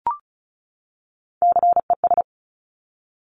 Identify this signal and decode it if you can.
Morse code, heard as a keyed tone, about 36 wpm, 700 Hz. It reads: CES
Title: Top Three Letter Combinations in Words 35wpm